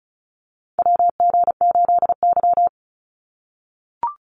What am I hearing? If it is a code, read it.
Morse code, heard as a keyed tone, about 34 wpm, 700 Hz. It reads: WG8Y
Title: NA SOTA Chaser Callsigns 35wpm